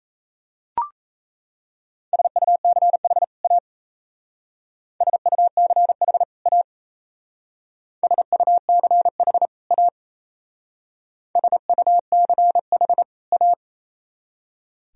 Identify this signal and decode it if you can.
Morse code, heard as a keyed tone, about 32 wpm, 700 Hz. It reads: SUCH A SUCH A SUCH A SUCH A
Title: Speed-Racing - Sets of 2 Words 28wpm